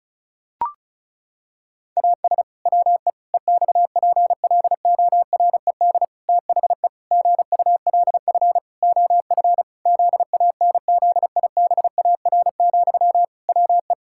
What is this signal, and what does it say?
Morse code, heard as a keyed tone, about 36 wpm, 700 Hz. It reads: AS WE EXPLORED THE GULF OF ZANZIBAR, WE
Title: Pangram Challenge v2 - 35wpm